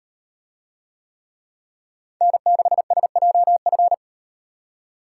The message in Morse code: N6SJF

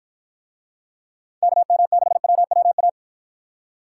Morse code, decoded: KM6WWA

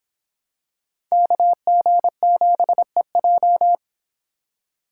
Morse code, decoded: KG7EJ

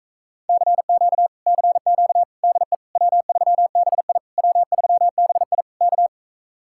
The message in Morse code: CQ CQ DE W3BI W3BI K

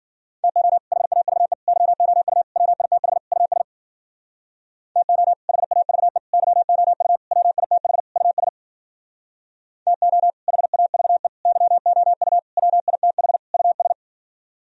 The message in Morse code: TO HAVE YOU WITH US TO HAVE YOU WITH US TO HAVE YOU WITH US

